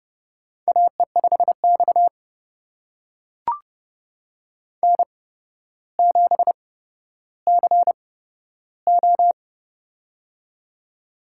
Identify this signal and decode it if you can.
Morse code, heard as a keyed tone, about 30 wpm, spaced as for 13 wpm, 700 Hz. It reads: AE5X N7CO
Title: USA Call Signs - 4 Characters - 8x Character Spacing - 30wpm